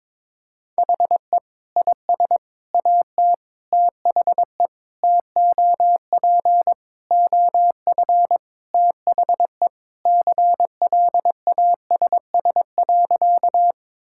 Morse code, heard as a keyed tone, about 22 wpm, 700 Hz. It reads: HE IS AT THE TOP OF THE CLASS.
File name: practice / 22wpm / Sentences with 3-Word Phrases - 22wpm